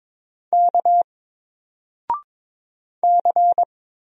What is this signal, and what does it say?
Morse code, heard as a keyed tone, about 22 wpm, 700 Hz. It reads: K C
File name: Single-Letter-Number-22wpm